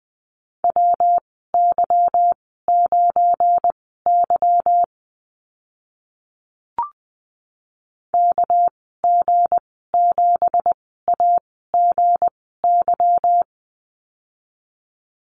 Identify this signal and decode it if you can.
Morse code, heard as a keyed tone, about 20 wpm, spaced as for 15 wpm, 700 Hz. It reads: WY9Y KG7AGY
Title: 219 Lesson 35g - Number 0 - Call Signs 20wpm